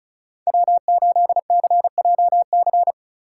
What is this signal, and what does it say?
Morse code, heard as a keyed tone, about 34 wpm, 700 Hz. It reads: W8CJC
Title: USA Call Signs - 5 Characters - 6x Character Spacing - 35wpm